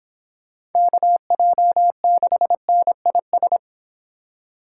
Morse code, heard as a keyed tone, about 26 wpm, 700 Hz. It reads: KJ6NIS